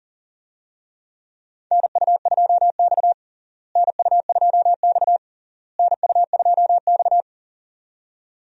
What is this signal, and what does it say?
Morse code, heard as a keyed tone, about 40 wpm, 700 Hz. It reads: NU2X NU2X NU2X
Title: USA Call Signs - 4 Characters - Repeated 3x - 40wpm